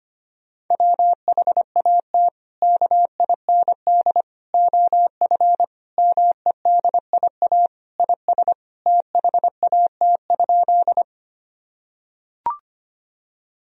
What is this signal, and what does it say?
Morse code, heard as a keyed tone, about 25 wpm, 700 Hz. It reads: WHAT KIND OF MEDIA IS THAT?
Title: Sentences from Top 1000 Words - Review Part 4 - 25wpm